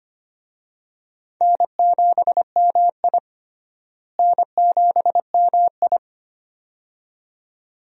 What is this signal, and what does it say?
Morse code, heard as a keyed tone, about 25 wpm, 700 Hz. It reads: N7MI N7MI
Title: USA Call Signs - 4 Characters - Repeated 2x - 25wpm